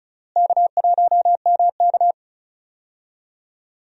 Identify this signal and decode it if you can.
Morse code, heard as a keyed tone, about 34 wpm, 700 Hz. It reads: K1MK
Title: USA Call Signs - 4 Characters - Repeated 3x - 35wpm